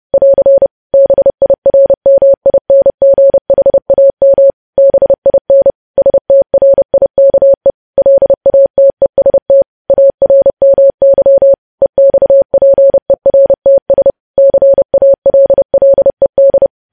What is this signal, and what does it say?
Morse code, heard as a keyed tone, about 30 wpm, 550 Hz. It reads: <AR> BIRMINGHAM BIN STRIKE LATEST ARMY EXPERTS CALLED
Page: news headlines in morse code at 30 wpm